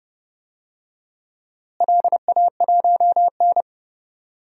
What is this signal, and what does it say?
Morse code, heard as a keyed tone, about 31 wpm, 700 Hz. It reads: LA1N
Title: DXCC-100 Call Signs - 4 Characters - Repeated 2x - 30wpm